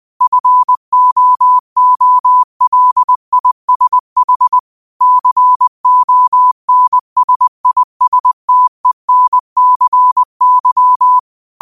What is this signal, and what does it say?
Morse code, heard as a keyed tone, about 20 wpm, 1000 Hz. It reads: FOOLISH CONSISTENCY
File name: qotdfiles / 20wpm